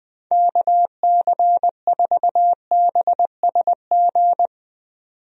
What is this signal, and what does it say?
Morse code, heard as a keyed tone, about 20 wpm, 700 Hz. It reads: KC4BSG
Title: USA Call Signs - 6 Characters - 20wpm